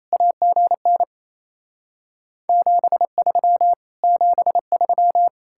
Morse code, heard as a keyed tone, about 38 wpm, 700 Hz. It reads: AGN 73 73